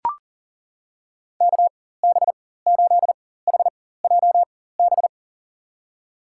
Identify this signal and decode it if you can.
Morse code, heard as a keyed tone, about 40 wpm, spaced as for 19 wpm, 700 Hz. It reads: KB8HJB
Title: USA Call Signs - 6 Characters - 4x Character Spacing - 40wpm